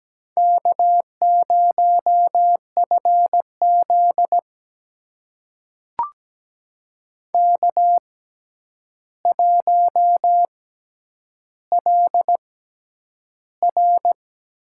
Morse code, heard as a keyed tone, about 17 wpm, spaced as for 9 wpm, 700 Hz. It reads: K0FZ K1LR
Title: USA Call Signs - 6x Character Spacing - 17wpm